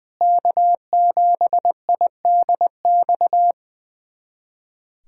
Morse code, heard as a keyed tone, about 21 wpm, 700 Hz. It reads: K7IDX
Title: USA Callsign DE Callsign 20wpm